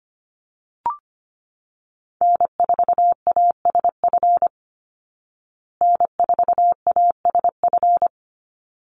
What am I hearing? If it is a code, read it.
Morse code, heard as a keyed tone, about 25 wpm, 700 Hz. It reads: N4ASF N4ASF
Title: Call Signs Repeated 3x - 25wpm